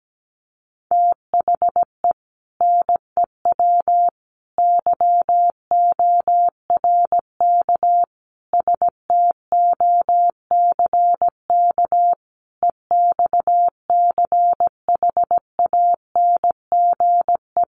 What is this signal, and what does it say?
Morse code, heard as a keyed tone, about 17 wpm, 700 Hz. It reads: THE NEW YORK STOCK EXCHANGE
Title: Sets of 5 Words 17wpm